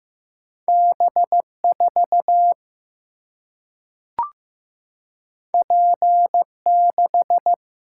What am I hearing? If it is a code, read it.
Morse code, heard as a keyed tone, about 15 wpm, 700 Hz. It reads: B 4 P 6
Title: Single-Letter Single-Number 15wpm